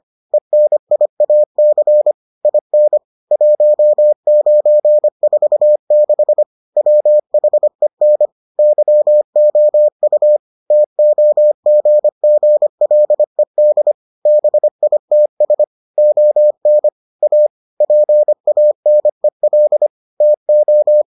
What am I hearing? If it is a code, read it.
Morse code, heard as a keyed tone, about 25 wpm, 600 Hz. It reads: ENIAC IN 1946 WHEN YOU TOGGLED BITS ON A PANEL TO